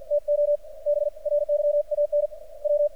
Morse code, heard as a keyed tone, about 45 wpm, 590 Hz. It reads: TO BRYAN W
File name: archived / CRSnet